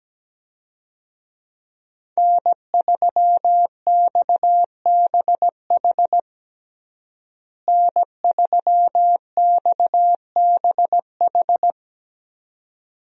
Morse code, heard as a keyed tone, about 17 wpm, 700 Hz. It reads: N3XBH N3XBH
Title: USA Call Signs - 5 Characters - Repeated 2x - 17wpm